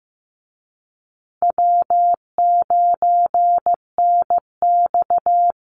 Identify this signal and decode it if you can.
Morse code, heard as a keyed tone, about 15 wpm, 700 Hz. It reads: W9NX
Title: Easy Call Signs Repeated 2x - 15wpm